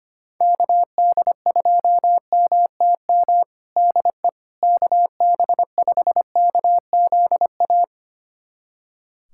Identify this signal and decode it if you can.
Morse code, heard as a keyed tone, about 25 wpm, 700 Hz. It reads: KD2MTM DE KB5KZA
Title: USA Callsign DE Callsign - 6 Characters - 25wpm